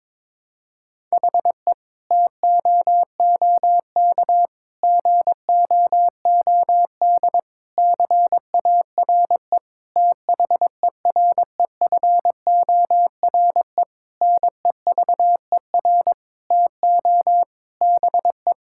Morse code, heard as a keyed tone, about 22 wpm, 700 Hz. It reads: HE TOOK GOOD CARE THEREFORE NEVER TO BE
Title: Sentences from Top 1000 Words - Review Part 5 - 22wpm